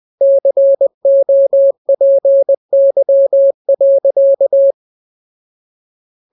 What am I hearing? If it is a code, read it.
Morse code, heard as a keyed tone, about 20 wpm, 550 Hz. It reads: COPY.
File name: jokes / joke 4 20 0000